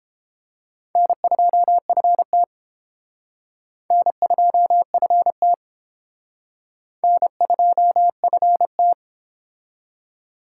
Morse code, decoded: N2FT N2FT N2FT

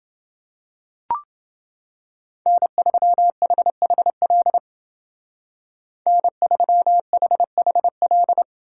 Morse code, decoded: N3HHL N3HHL

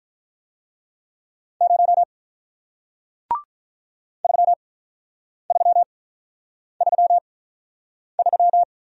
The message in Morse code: 0 3333

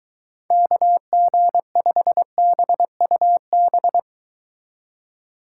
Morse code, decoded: KG5BUB

